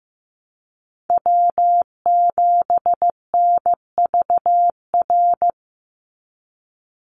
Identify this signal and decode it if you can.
Morse code, heard as a keyed tone, about 15 wpm, 700 Hz. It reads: W7NVR
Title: USA Call Signs 15wpm